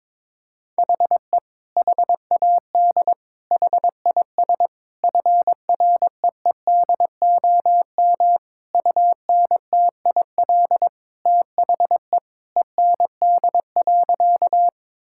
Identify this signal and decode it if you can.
Morse code, heard as a keyed tone, about 22 wpm, 700 Hz. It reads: HE HAD HIS FREEDOM UNTIL THE END.
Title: Sentences from Top 1000 Words - Review Part 5 - 22wpm